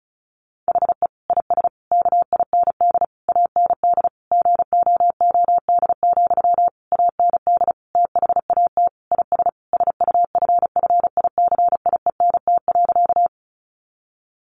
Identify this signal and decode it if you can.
Morse code, heard as a keyed tone, about 34 wpm, 700 Hz. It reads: HE IS KIND AND GOOD, AND THAT IS SUFFICIENT.
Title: Sentences from Top 1400 Words 35wpm